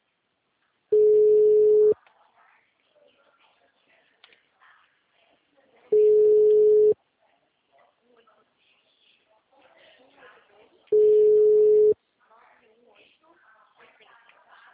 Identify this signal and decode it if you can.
Ringback tone (Europe, 425 Hz), with speech.